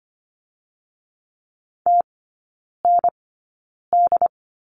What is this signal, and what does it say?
Morse code, heard as a keyed tone, about 25 wpm, spaced as for 9 wpm, 700 Hz. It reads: TND
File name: Lesson 14e - Number 2 - Three Characters 25wpm